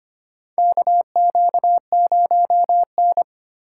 Morse code, heard as a keyed tone, about 25 wpm, 700 Hz. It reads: KQ0N
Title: USA Call Signs - 4 Characters - 4x Character Spacing - 25wpm